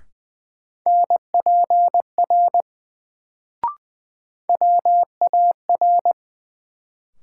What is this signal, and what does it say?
Morse code, heard as a keyed tone, about 20 wpm, 700 Hz. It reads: NPR WAR
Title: Sets of 5 Words - New Word Drill - 20wpm